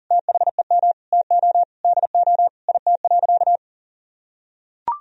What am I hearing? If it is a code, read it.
Morse code, heard as a keyed tone, about 40 wpm, 700 Hz. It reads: THEM TO DO IT.
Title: Sentences from Top 1000 Words - Review Part 2 - 40wpm